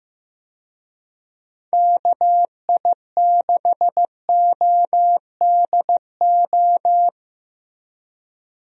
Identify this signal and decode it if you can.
Morse code, heard as a keyed tone, about 15 wpm, 700 Hz. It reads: KI6ODO
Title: USA Call Signs - 6 Characters - Repeated 3x - 15wpm